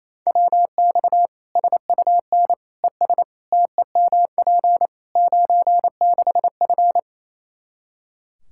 Morse code, heard as a keyed tone, about 28 wpm, 700 Hz. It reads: WX SUN ES TEMP 96F
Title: Practice QSO - Exchange 2 - Info - 28wpm